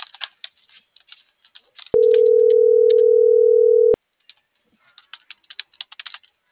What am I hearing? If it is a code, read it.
Ringback tone (North America), 440 + 480 Hz.
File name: CENTRAL NMP / out melanie.diaznmp - .38330